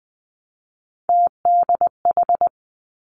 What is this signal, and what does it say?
Morse code, heard as a keyed tone, about 20 wpm, 700 Hz. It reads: TDH
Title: Three Challenging Letter-Number Combinations 20wpm